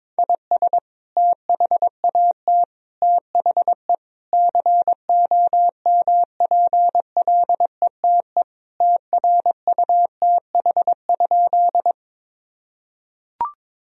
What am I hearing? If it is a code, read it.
Morse code, heard as a keyed tone, about 22 wpm, 700 Hz. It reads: IS THAT THE COMPLETE TRUTH?
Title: Sentences from Top 1000 Words - Review Part 2 - 22wpm